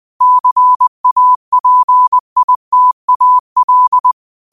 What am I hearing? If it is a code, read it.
Morse code, heard as a keyed tone, about 20 wpm, 1000 Hz. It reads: CAPITAL
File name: qotdfiles / 20wpm